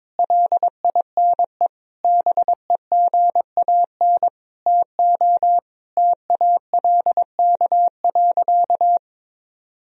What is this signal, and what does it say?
Morse code, heard as a keyed tone, about 22 wpm, 700 Hz. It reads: LINE BEGAN TO TALK.
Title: Sentences from Top 1000 Words - Review Part 1 - 22wpm